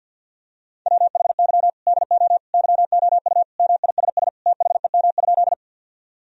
Morse code, heard as a keyed tone, about 46 wpm, 700 Hz. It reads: WHY DO YOU MISS THEM?
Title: Sentences from Top 1000 Words - Review Part 3 - 50wpm